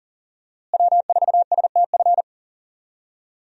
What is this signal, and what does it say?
Morse code, heard as a keyed tone, about 37 wpm, 700 Hz. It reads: W4STF
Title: USA Call Signs - 5 Characters - Repeated 2x - 40wpm